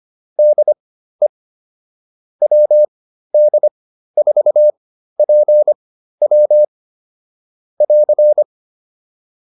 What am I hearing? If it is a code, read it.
Morse code, heard as a keyed tone, about 25 wpm, spaced as for 13 wpm, 600 Hz. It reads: DE WD4PW <AR>
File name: QSO303 13